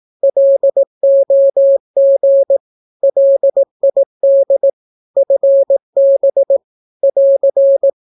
Morse code, heard as a keyed tone, about 18 wpm, 550 Hz. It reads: LOG LID FB <AR>
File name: naukapodstaw / skroty / skroty